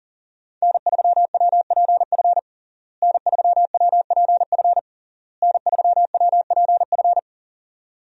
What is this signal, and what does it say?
Morse code, heard as a keyed tone, about 40 wpm, 700 Hz. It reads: N3WPF N3WPF N3WPF
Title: USA Call Signs - 5 Characters - Repeated 3x - 40wpm